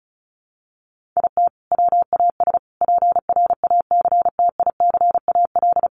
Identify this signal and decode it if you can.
Morse code, heard as a keyed tone, about 34 wpm, 700 Hz. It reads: IT WAS PRACTICAL
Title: Sentences from Top 1400 Words 35wpm